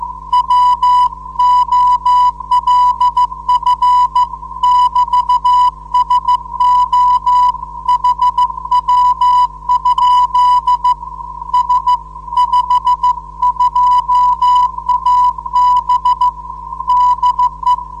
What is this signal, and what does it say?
Morse code, heard as a keyed tone, about 14 wpm, 990 Hz. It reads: WOLF <BT> SO HW? S52AB DE
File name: 472khz qso web / 29march2014 / 01 dl4yhf 29mar2014 2248utc